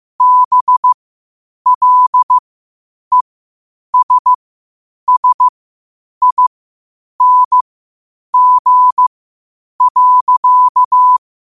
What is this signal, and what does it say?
Morse code, heard as a keyed tone, about 15 wpm, spaced as for 8 wpm, 1000 Hz. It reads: BLESSING.
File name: 8wpm